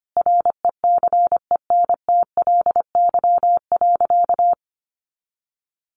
Morse code, heard as a keyed tone, about 25 wpm, 700 Hz. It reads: RECENTLY.
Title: Sentences from Top 1200 Words 25wpm